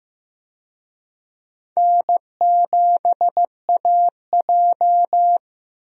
Morse code, heard as a keyed tone, about 15 wpm, 700 Hz. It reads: N7AJ
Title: USA Call Signs - 4 Characters - Repeated 2x - 15wpm